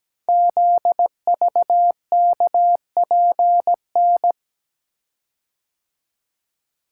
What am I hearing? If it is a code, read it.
Morse code, heard as a keyed tone, about 17 wpm, 700 Hz. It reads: ZVKPN
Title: Code Groups - 5 Letters - 17wpm